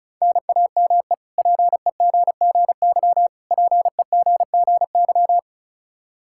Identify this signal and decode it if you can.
Morse code, heard as a keyed tone, about 34 wpm, 700 Hz. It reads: NAME PEGGY PEGGY
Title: K1USN SST QSO Practice 35wpm